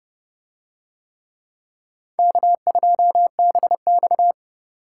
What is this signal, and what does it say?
Morse code, heard as a keyed tone, about 28 wpm, 700 Hz. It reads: K2BX